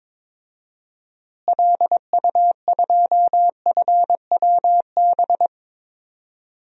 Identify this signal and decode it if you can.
Morse code, heard as a keyed tone, about 22 wpm, 700 Hz. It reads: LU2FWB
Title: DXCC-100 Call Signs - 6 Characters - Repeated 2x - 22wpm